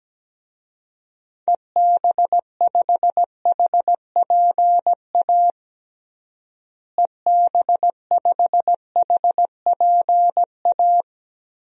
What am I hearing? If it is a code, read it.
Morse code, heard as a keyed tone, about 17 wpm, 700 Hz. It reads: EB5HPA EB5HPA
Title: DXCC-100 Call Signs - 6 Characters - Repeated 3x - 17wpm